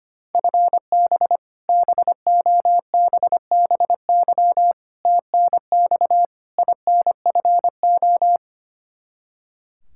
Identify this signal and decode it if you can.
Morse code, heard as a keyed tone, about 25 wpm, 700 Hz. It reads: FB BOBBY TNX INFO
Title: Practice QSO - Exchange 3 - More Info - 25wpm